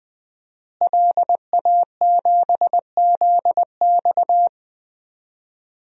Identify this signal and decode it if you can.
Morse code, heard as a keyed tone, about 21 wpm, 700 Hz. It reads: LA7ZX
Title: DXCC-100 Call Signs - 5 Characters - Repeated 2x - 20wpm